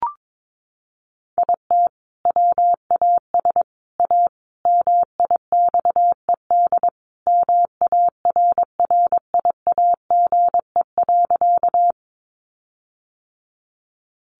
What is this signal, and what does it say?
Morse code, heard as a keyed tone, about 22 wpm, 700 Hz. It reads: IT WAS A MIXED MARRIAGE.
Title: Sentences from Top 1500 Words 22wpm